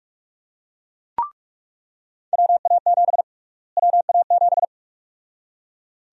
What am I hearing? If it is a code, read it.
Morse code, heard as a keyed tone, about 42 wpm, 700 Hz. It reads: WA7 WA7
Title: USA Call Sign Prefixes - Repeated 2x - 45wpm